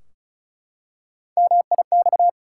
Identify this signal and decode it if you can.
Morse code, heard as a keyed tone, about 36 wpm, 700 Hz. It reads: MIX
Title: Three Letter Words - 35wpm